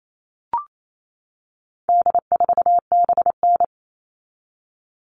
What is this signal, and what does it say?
Morse code, heard as a keyed tone, about 28 wpm, 700 Hz. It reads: D4BN
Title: All Cut Numbers - 4-Digit 28wpm